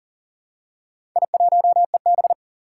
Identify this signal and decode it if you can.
Morse code, heard as a keyed tone, about 40 wpm, 700 Hz. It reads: I1EB
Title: Four Challenging Letter-Number Combinations 40wpm